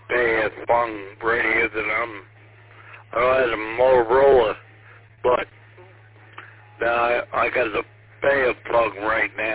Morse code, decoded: none

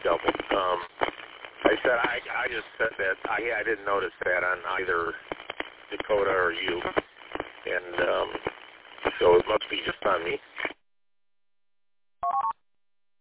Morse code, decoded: none

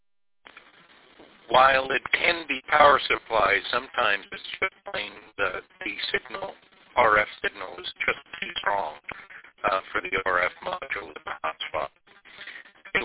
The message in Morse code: none

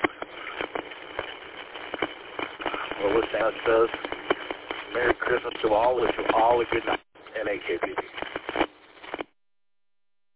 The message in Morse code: none

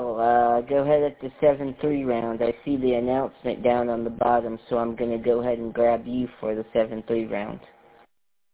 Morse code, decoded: none

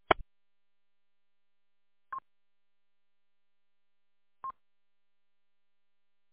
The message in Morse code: none

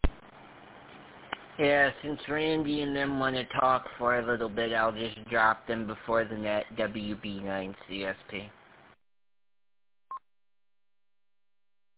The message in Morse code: none

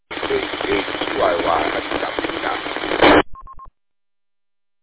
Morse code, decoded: none

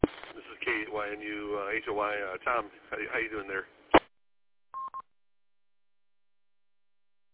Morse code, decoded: none

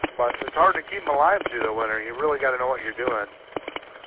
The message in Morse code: none

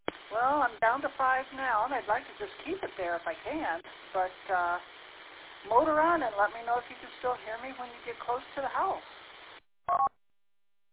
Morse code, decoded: none

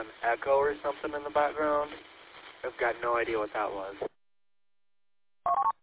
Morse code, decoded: none

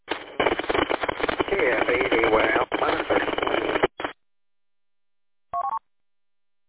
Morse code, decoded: none

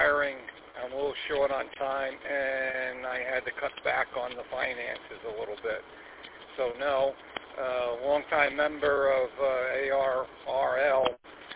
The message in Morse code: none

